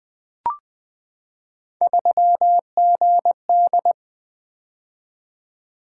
Morse code, heard as a keyed tone, about 20 wpm, 700 Hz. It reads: 3GD
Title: Three Letter-Number Combinations - 20wpm